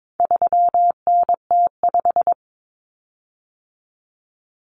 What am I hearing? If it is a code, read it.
Morse code, heard as a keyed tone, about 22 wpm, 700 Hz. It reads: 3NT5
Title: Common Cut Numbers - 4-Digit 22wpm